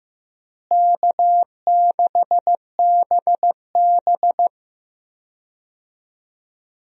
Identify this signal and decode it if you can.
Morse code, heard as a keyed tone, about 15 wpm, 700 Hz. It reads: K6BB
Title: USA Call Signs - 4 Characters - Repeated 3x - 15wpm